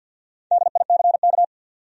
Morse code, heard as a keyed tone, about 50 wpm, 700 Hz. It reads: DICK
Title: USA Common Names - Max 5 Letters - 50wpm